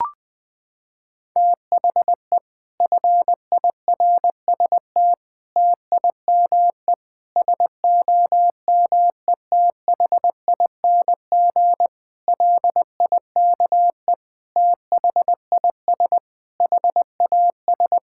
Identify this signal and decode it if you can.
Morse code, heard as a keyed tone, about 20 wpm, 700 Hz. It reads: THE FIRST TIME SOMETHING LIKE THIS HAS
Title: Sets of 8 Words - 20wpm